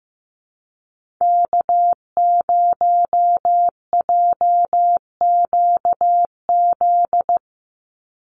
Morse code, heard as a keyed tone, about 15 wpm, 700 Hz. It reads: K0JQZ
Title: NA SOTA Activator Callsigns 15wpm